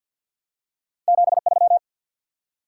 Morse code, heard as a keyed tone, about 53 wpm, 700 Hz. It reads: 73
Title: Top 90 QSO Elements 50wpm